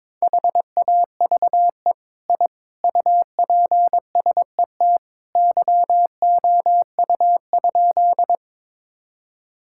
Morse code, heard as a keyed tone, about 21 wpm, 700 Hz. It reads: HAVE I UPSET YOU?
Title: Sentences from Top 1000 Words - Review Part 5 - 22wpm